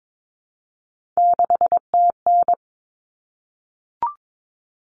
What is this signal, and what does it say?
Morse code, heard as a keyed tone, about 22 wpm, 700 Hz. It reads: 6TN
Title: Common Cut Numbers - 3-Digit 22wpm